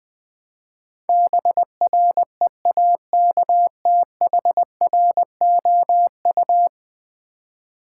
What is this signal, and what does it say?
Morse code, heard as a keyed tone, about 20 wpm, 700 Hz. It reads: BREAKTHROU
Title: Build A Word - 5 Letters At A Time - 12-Letter Words - 20wpm